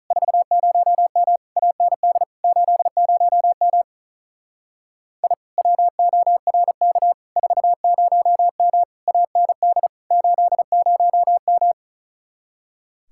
Morse code, heard as a keyed tone, about 43 wpm, 700 Hz. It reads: 40M AND 80M I WORK 40M AND 80M
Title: Speed-Racing Common QSO Phrases - 35wpm